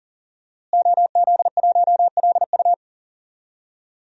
Morse code, heard as a keyed tone, about 40 wpm, 700 Hz. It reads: OZ1LU
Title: DXCC-100 Call Signs - 5 Characters - Repeated 2x - 40wpm